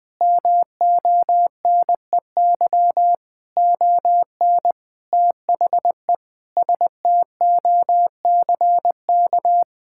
Morse code, heard as a keyed tone, about 20 wpm, 700 Hz. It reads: MONEY ON THE STOCK
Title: Sentences from Top 1000 Words - Review Part 6 - 20wpm